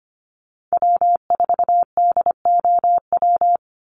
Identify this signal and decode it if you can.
Morse code, heard as a keyed tone, about 25 wpm, 700 Hz. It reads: W4DOW
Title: NA SOTA Chaser Callsigns 25wpm